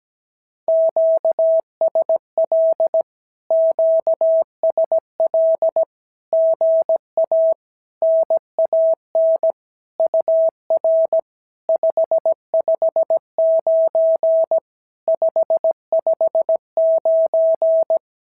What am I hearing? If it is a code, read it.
Morse code, heard as a keyed tone, about 17 wpm, 650 Hz. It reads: QSL QSL GA NAN UR 559 559
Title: SOTA QSO Practice 17wpm